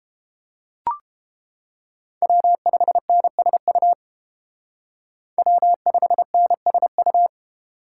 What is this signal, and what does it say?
Morse code, heard as a keyed tone, about 30 wpm, 700 Hz. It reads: W5NSU W5NSU